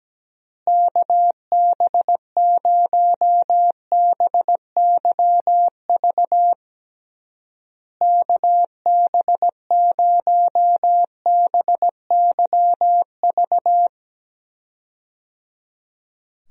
Morse code, heard as a keyed tone, about 17 wpm, 700 Hz. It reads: KB0BYV KB0BYV
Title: USA Call Signs - 6 Characters - Repeated 3x - 17wpm